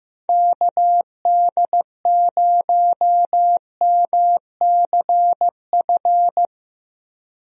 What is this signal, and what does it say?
Morse code, heard as a keyed tone, about 16 wpm, 700 Hz. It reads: KD0MCF
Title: USA Call Signs - 6 Characters - Repeated 2x - 15wpm